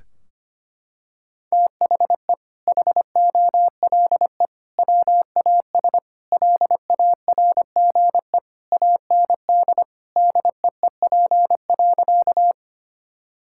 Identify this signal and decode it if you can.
Morse code, heard as a keyed tone, about 25 wpm, 700 Hz. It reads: THE HOLE WAS LARGE AND DEEP.
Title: Sentences from Top 1600 Words 25wpm